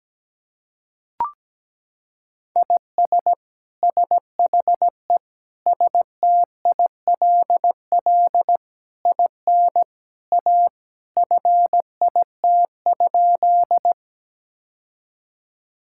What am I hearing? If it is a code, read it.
Morse code, heard as a keyed tone, about 17 wpm, 700 Hz. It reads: IS SHE STILL IN A FIT?
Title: Sentences from Top 1000 Words - Review Part 5 - 17wpm